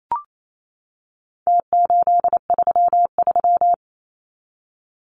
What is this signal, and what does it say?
Morse code, heard as a keyed tone, about 28 wpm, 700 Hz. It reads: T833
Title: Numbers - Contest Style - 4-Digit 28wpm